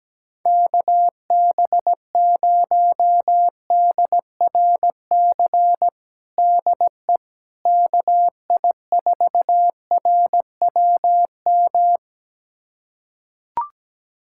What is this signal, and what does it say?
Morse code, heard as a keyed tone, about 17 wpm, 700 Hz. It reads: KB0DRC DE KI4RWM
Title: USA Callsign DE Callsign - 6 Characters - 17wpm